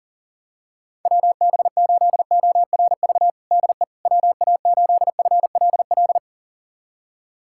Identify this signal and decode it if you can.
Morse code, heard as a keyed tone, about 40 wpm, 700 Hz. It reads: WB8ORV DE WA8FLL
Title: USA Callsign DE Callsign - 6 Characters - 40wpm